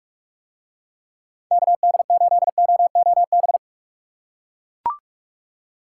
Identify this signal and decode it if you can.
Morse code, heard as a keyed tone, about 41 wpm, 700 Hz. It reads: KD8OOB